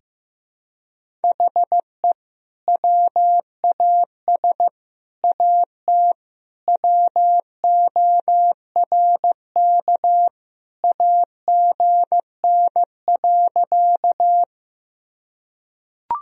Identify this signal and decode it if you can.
Morse code, heard as a keyed tone, about 15 wpm, 700 Hz. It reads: HE WAS AT WORK AGN.
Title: Sentences from Top 200 Words and Abbreviations 15wpm